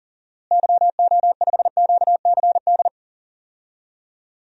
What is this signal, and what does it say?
Morse code, heard as a keyed tone, about 40 wpm, 700 Hz. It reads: YO5QCD
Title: DXCC-100 Call Signs - 6 Characters - Repeated 2x - 40wpm